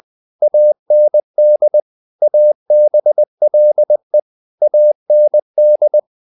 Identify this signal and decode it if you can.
Morse code, heard as a keyed tone, about 20 wpm, 600 Hz. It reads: AND ABLE AND
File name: the old man and the sea part 2 20wpm